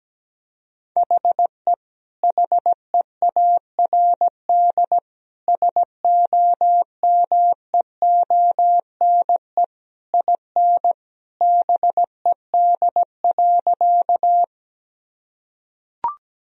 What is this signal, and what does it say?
Morse code, heard as a keyed tone, about 17 wpm, 700 Hz. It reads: HE HEARD SOMEONE IN BED.
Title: Sentences from Top 1000 Words - Review Part 4 - 17wpm